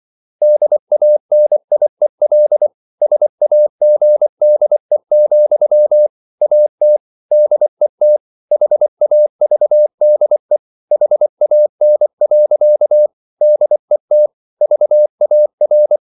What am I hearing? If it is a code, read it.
Morse code, heard as a keyed tone, about 24 wpm, 600 Hz. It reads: DANIEL SAGDE, AT DET HAVDE HAN. DET VAR
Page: Ferie 24 wpm